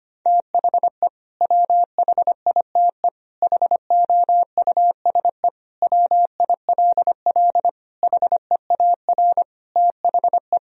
Morse code, heard as a keyed tone, about 25 wpm, 700 Hz. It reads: THE WHITE HOUSE WILL HEAR THE
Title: Sentences from Top 1000 Words - Review Part 1 - 25wpm